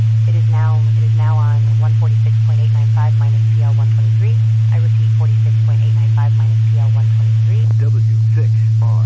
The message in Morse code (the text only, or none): none